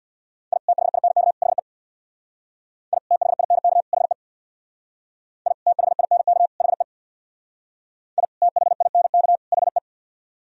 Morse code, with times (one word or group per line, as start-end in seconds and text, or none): 0.5-0.6 I
0.7-1.3 THINK
1.4-1.6 HE
2.9-3.0 I
3.1-3.8 THINK
3.9-4.1 HE
5.5-5.5 I
5.7-6.5 THINK
6.6-6.8 HE
8.2-8.3 I
8.4-9.4 THINK
9.5-9.8 HE